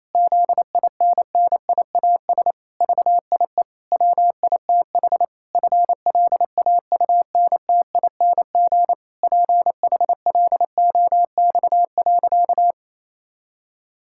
Morse code, heard as a keyed tone, about 27 wpm, 700 Hz. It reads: ZINNIAS VIE WITH FLAUNTING PHLOX.
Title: Pangram Challenge v2 - 28wpm